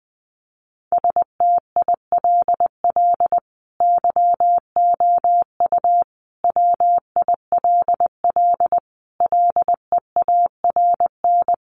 Morse code, heard as a keyed tone, about 20 wpm, 700 Hz. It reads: STILL YOU WILL LEARN
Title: Sentences from Top 600 Words 20wpm